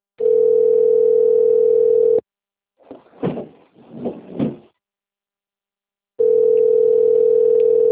Ringback tone (North America), 440 + 480 Hz.
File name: rg-1600-unknown- - - .42320